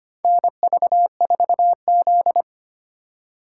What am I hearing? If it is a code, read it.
Morse code, heard as a keyed tone, about 25 wpm, 690 Hz. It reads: NV4Z
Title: USA Call Signs - 4 Characters - 4x Character Spacing - 25wpm